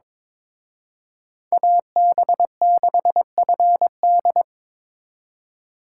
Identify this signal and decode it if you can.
Morse code, heard as a keyed tone, about 27 wpm, 700 Hz. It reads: AB6FD
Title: Speed Racing - USA Call Signs 22wpm